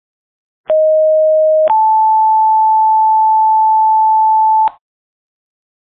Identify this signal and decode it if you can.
Two-tone page, heard as a one-second tone, then a three-second tone, about 620 then 880 Hz. It is heard from 0.7 to 4.7 s.
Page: play before STATION EMS tone